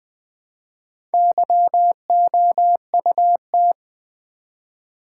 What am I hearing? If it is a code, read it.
Morse code, heard as a keyed tone, about 20 wpm, 700 Hz. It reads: YOUT
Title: Build A Word - 1 Letter At A Time - 5-Letter Words - 20wpm